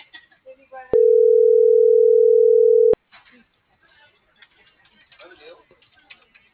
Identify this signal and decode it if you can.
Ringback tone (North America), 440 + 480 Hz, with speech.
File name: CENTRAL AGUILERA